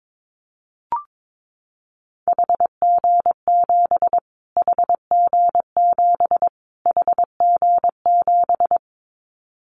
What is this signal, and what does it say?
Morse code, heard as a keyed tone, about 22 wpm, 700 Hz. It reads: HG7 HG7 HG7
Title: Callsign Prefixes for Top 25 DXCC - Repeated 3x - 22wpm